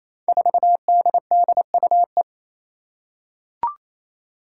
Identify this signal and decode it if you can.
Morse code, heard as a keyed tone, about 27 wpm, 700 Hz. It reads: 4DDUE
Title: Five Challenging Letter-Number Combinations 28wpm